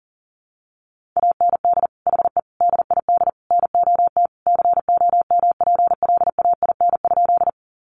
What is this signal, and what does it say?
Morse code, heard as a keyed tone, about 40 wpm, 700 Hz. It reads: AND HE DID NOT COMPLAIN?
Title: Sentences from Top 1300 Words 40wpm